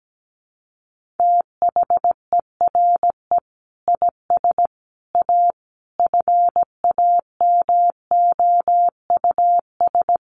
Morse code, heard as a keyed tone, about 17 wpm, 700 Hz. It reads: THERE IS A FAMOUS
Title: Sentences from Top 1500 Words 17wpm